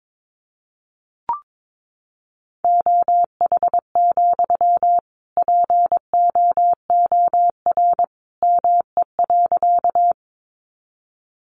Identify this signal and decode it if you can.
Morse code, heard as a keyed tone, about 22 wpm, 700 Hz. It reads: OH, POOR ME.
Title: Sentences from Top 1000 Words 22wpm